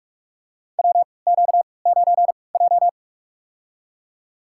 Morse code, heard as a keyed tone, about 47 wpm, spaced as for 26 wpm, 700 Hz. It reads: WQ9J
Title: USA Call Signs - 4 Characters - 3x Character Spacing - 45wpm